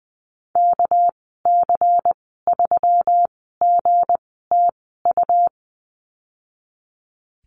Morse code, heard as a keyed tone, about 20 wpm, spaced as for 15 wpm, 700 Hz. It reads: KC3GTU
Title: 200 Lesson 32g - Letter k - Call Signs 20wpm